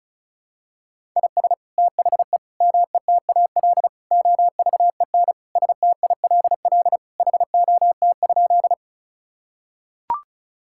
Morse code, heard as a keyed tone, about 36 wpm, 700 Hz. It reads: IS THE METAL OVEN STILL HOT?
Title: Sentences from Top 1000 Words - Review Part 6 - 35wpm